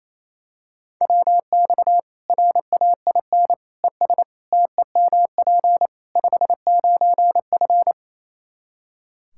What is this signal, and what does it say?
Morse code, heard as a keyed tone, about 28 wpm, 700 Hz. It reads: WX RAIN ES TEMP 59F
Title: Practice QSO - Exchange 2 - Info - 28wpm